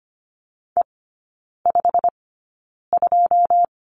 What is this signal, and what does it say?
Morse code, heard as a keyed tone, about 25 wpm, spaced as for 9 wpm, 700 Hz. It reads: E52
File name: course / 25:10wpm / 093 Lesson 16e - Letter c - Three Characters 25wpm